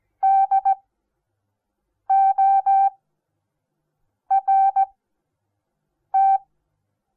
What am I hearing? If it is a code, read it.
Morse code, heard as a keyed tone, about 17 wpm, spaced as for 6 wpm, 780 Hz. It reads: DORT
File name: Onlinelehrgang / morsen / DJ4UF Morsen08